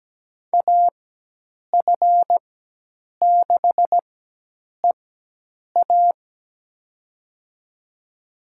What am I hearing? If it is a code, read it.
Morse code, heard as a keyed tone, about 17 wpm, spaced as for 8 wpm, 700 Hz. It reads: AF6EA
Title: USA Call Signs - 5 Characters - 4x Character Spacing - 17wpm